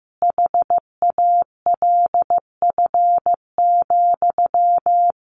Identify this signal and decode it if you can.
Morse code, heard as a keyed tone, about 15 wpm, 700 Hz. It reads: HALF,
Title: A Sound Of Thunder 15wpm